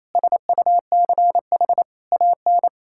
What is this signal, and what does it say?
Morse code, heard as a keyed tone, about 28 wpm, 700 Hz. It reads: SUCH AN
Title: Sentences from Top 1600 Words 28wpm